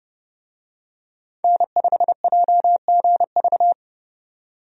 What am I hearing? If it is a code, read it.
Morse code, heard as a keyed tone, about 30 wpm, 700 Hz. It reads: N5JGV
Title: USA Call Signs - 5 Characters - 30wpm